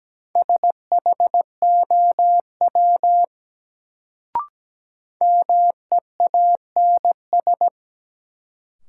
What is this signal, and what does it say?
Morse code, heard as a keyed tone, about 17 wpm, 700 Hz. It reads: SHOW MEANS
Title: Sets of 5 Words - New Word Drill - 17wpm